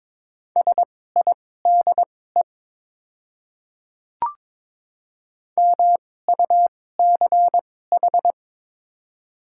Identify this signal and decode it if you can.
Morse code, heard as a keyed tone, about 22 wpm, spaced as for 16 wpm, 700 Hz. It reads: SIDE MUCH
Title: 2x Character Spacing - Top 200 Words - 22wpm